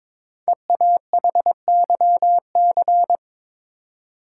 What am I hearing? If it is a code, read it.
Morse code, heard as a keyed tone, about 22 wpm, 700 Hz. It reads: EAHYC
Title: Code Groups - 5 Letters - 22wpm